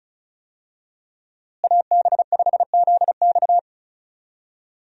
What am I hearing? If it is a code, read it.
Morse code, heard as a keyed tone, about 36 wpm, 700 Hz. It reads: AB5ZX
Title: USA Call Signs - 5 Characters - Repeated 3x - 35wpm